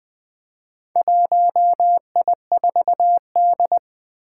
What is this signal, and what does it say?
Morse code, heard as a keyed tone, about 20 wpm, 700 Hz. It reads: 1I4D
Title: Four Challenging Letter-Number Combinations 20wpm